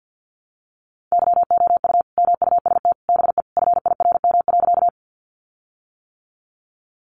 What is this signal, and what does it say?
Morse code, heard as a keyed tone, about 50 wpm, 700 Hz. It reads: YOU MUST BE FIRM.
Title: Sentences from Top 1400 Words 50wpm